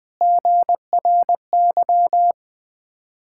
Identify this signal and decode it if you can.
Morse code, heard as a keyed tone, about 20 wpm, 700 Hz. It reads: GRY
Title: Ontario QSO Party Exchange 20wpm